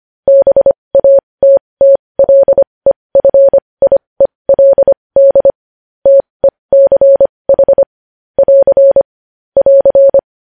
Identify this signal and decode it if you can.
Morse code, heard as a keyed tone, about 25 wpm, spaced as for 20 wpm, 550 Hz. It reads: BATTLEFIELD TECH <AR> <AR>
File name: podcast / News headlines-1x-25x20